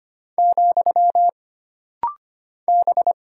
Morse code, heard as a keyed tone, about 25 wpm, 700 Hz. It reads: , B
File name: Single Character (Letter:Number:Punctuation:Prosign) - Rapid-Fire - 25wpm